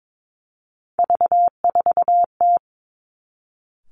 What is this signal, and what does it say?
Morse code, heard as a keyed tone, about 22 wpm, 700 Hz. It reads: V4T
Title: Three Challenging Letter-Number Combinations 22wpm